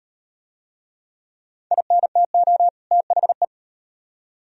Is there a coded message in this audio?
Yes: Morse code from 1.7 to 3.5 s, heard as a keyed tone, about 38 wpm, 700 Hz. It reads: INTO THE